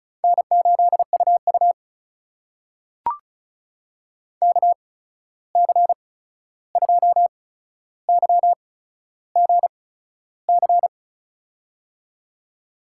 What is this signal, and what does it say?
Morse code, heard as a keyed tone, about 36 wpm, spaced as for 13 wpm, 700 Hz. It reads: N8UU KC2YGC
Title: USA Call Signs - 8x Character Spacing - 35wpm